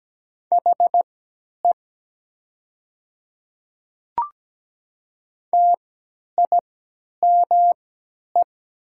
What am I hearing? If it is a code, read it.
Morse code, heard as a keyed tone, about 17 wpm, spaced as for 10 wpm, 700 Hz. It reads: HE TIME
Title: 3x Character Spacing - Top 300 Words - 17wpm